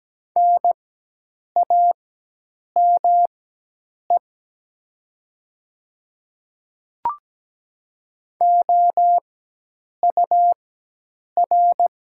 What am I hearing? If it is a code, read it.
Morse code, heard as a keyed tone, about 17 wpm, spaced as for 8 wpm, 700 Hz. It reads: NAME OUR
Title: 4x Character Spacing - Top 500 Words - 17wpm